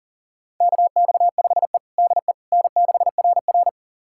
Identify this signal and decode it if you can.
Morse code, heard as a keyed tone, about 40 wpm, 700 Hz. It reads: KX5E DE N6RR
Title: USA Callsign DE Callsign - 4 Characters - 40wpm